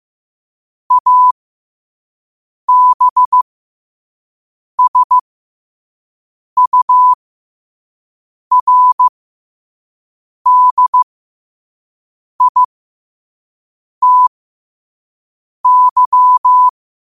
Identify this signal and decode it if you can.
Morse code, heard as a keyed tone, about 15 wpm, spaced as for 5 wpm, 1000 Hz. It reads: ABSURDITY
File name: qotdfiles / 5wpm